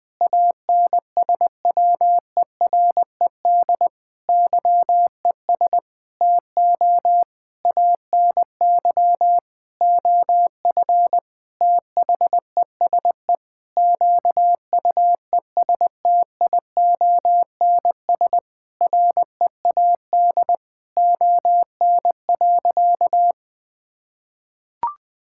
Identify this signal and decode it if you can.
Morse code, heard as a keyed tone, about 20 wpm, 700 Hz. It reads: ANSWERED YES TO ANY OF THESE QUESTIONS READ ON.
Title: Sentences with 9-Word Phrases - Spoken Beginning - 20wpm